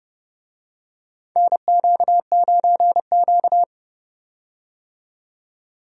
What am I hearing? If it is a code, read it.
Morse code, heard as a keyed tone, about 32 wpm, 700 Hz. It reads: NQ9Q
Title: USA Call Signs - 4 Characters - 30wpm